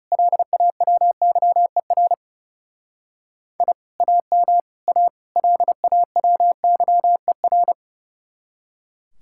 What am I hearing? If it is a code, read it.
Morse code, heard as a keyed tone, about 36 wpm, 700 Hz. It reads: LAWYER I AM A LAWYER
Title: Speed-Racing Common QSO Phrases - 30wpm